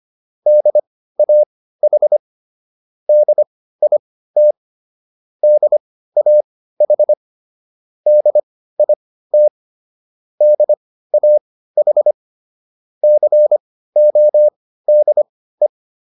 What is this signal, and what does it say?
Morse code, heard as a keyed tone, about 25 wpm, spaced as for 15 wpm, 600 Hz. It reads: DAH DIT DAH DIT DAH CODE